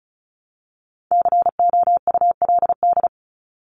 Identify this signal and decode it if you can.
Morse code, heard as a keyed tone, about 34 wpm, 700 Hz. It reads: COULD
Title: Top 200 Words Repeated 35wpm